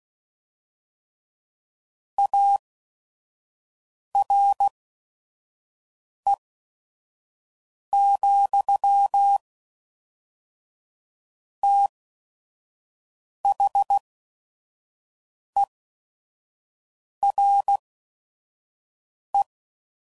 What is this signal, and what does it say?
Morse code, heard as a keyed tone, about 16 wpm, spaced as for 5 wpm, 800 Hz. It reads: ARE, THERE
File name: CW Code Groups @5WPM / Technical Subjects / AND Gate